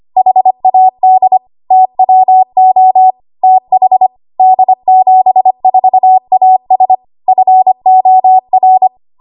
Morse code, heard as a keyed tone, about 25 wpm, 750 Hz. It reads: HAD TWO TH D74AS FOR
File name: Archive / 25 WPM / 25WPM